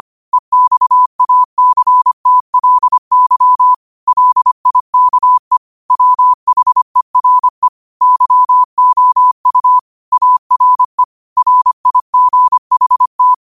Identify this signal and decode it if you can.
Morse code, heard as a keyed tone, about 25 wpm, 1000 Hz. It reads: EXACTLY LIKE WHERE YOU ARE RIGHT
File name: qotdfiles / 25wpm